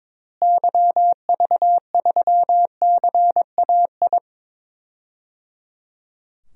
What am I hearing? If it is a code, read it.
Morse code, heard as a keyed tone, about 22 wpm, 700 Hz. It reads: YV3CAI
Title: DXCC-100 Call Signs - 6 Characters - Repeated 3x - 22wpm